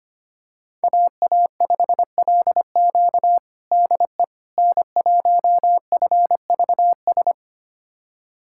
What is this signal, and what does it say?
Morse code, heard as a keyed tone, about 25 wpm, 700 Hz. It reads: AA5LQ DE N1FVS
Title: USA Callsign DE Callsign - 5 Characters - 25wpm